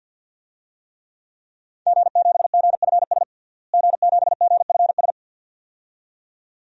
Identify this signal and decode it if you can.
Morse code, heard as a keyed tone, about 50 wpm, 700 Hz. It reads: G7GFS G7GFS
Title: DXCC-100 Call Signs - 5 Characters - Repeated 2x - 50wpm